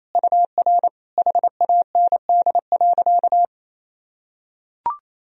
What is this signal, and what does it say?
Morse code, heard as a keyed tone, about 28 wpm, 700 Hz. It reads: UR HAND.
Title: Sentences from Top 400 Words and Abbreviations 28wpm